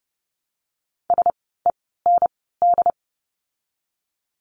Morse code, heard as a keyed tone, about 30 wpm, 700 Hz. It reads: S E N D
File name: Lesson 16f - Letter c - Words 30wpm